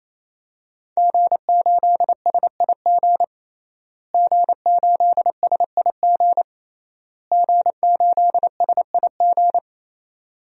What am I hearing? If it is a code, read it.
Morse code, heard as a keyed tone, about 27 wpm, 700 Hz. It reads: G8SIG G8SIG G8SIG
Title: DXCC-100 Call Signs - 5 Characters - Repeated 3x - 28wpm